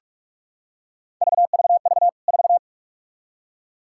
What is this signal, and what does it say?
Morse code, heard as a keyed tone, about 46 wpm, 700 Hz. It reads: VVV 4
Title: K1USN SST QSO Practice 45wpm